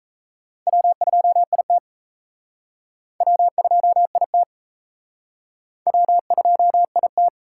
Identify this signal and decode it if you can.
Morse code, heard as a keyed tone, about 35 wpm, 700 Hz. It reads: W2IT W2IT W2IT